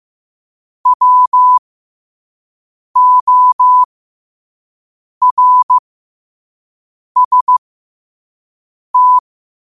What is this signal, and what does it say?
Morse code, heard as a keyed tone, about 15 wpm, spaced as for 5 wpm, 1000 Hz. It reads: WORST
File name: qotdfiles / 5wpm